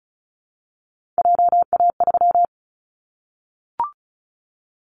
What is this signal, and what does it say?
Morse code, heard as a keyed tone, about 36 wpm, 700 Hz. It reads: JA3
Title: Callsign Prefixes for Top 50 DXCC - 35wpm